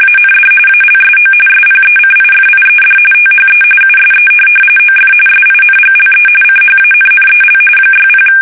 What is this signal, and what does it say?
RTTY, about 45.45 baud. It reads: AND ALSO SPECIFIED THE USE OF SINGLE CHANNEL 60
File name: fsk850 / teletype-RTTY